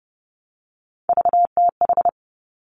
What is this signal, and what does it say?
Morse code, heard as a keyed tone, about 30 wpm, 700 Hz. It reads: VTH
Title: Three Challenging Letter-Number Combinations 30wpm